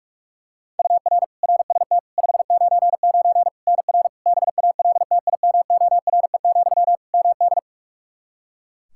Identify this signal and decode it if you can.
Morse code, heard as a keyed tone, about 46 wpm, 700 Hz. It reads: UR RST 599 NR BALTIMORE, MD
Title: Practice QSO - Exchange 1 - Beginning - 45wpm